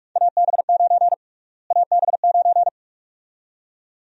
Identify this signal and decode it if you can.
Morse code, heard as a keyed tone, about 42 wpm, 700 Hz. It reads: AB9 AB9
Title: USA Call Sign Prefixes - Repeated 2x - 45wpm